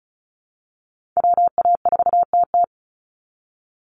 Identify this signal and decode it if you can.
Morse code, heard as a keyed tone, about 34 wpm, 700 Hz. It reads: WA4TT
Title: Call Signs Repeated 2x - 35wpm